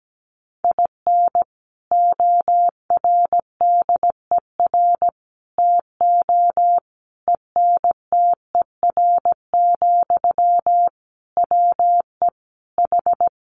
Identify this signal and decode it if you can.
Morse code, heard as a keyed tone, about 17 wpm, 700 Hz. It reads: IN ORDER TO ENTER, WE H
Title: Sentences from Top 1300 Words 17wpm